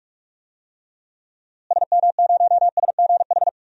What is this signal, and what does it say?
Morse code, heard as a keyed tone, about 43 wpm, 700 Hz. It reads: SM0SGH
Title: DXCC-100 Call Signs - 6 Characters - Repeated 2x - 45wpm